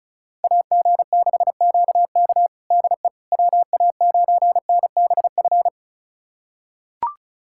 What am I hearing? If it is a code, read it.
Morse code, heard as a keyed tone, about 34 wpm, 700 Hz. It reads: AG6QK DE WA9NBF
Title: USA Callsign DE Callsign 35wpm